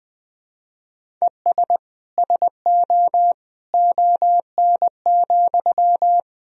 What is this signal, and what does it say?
Morse code, heard as a keyed tone, about 20 wpm, 700 Hz. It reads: ES SO ON,
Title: Sentences from Top 100 Words and Abbreviations 20wpm